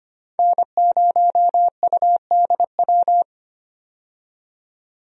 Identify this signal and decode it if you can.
Morse code, heard as a keyed tone, about 25 wpm, 700 Hz. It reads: N0UDW
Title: USA Call Signs - 5 Characters - 25wpm